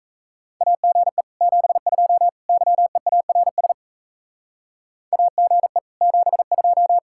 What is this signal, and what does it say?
Morse code, heard as a keyed tone, about 34 wpm, 700 Hz. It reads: AGE 72 YEARS AGE 72
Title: Speed-Racing Common QSO Phrases Encore - 28wpm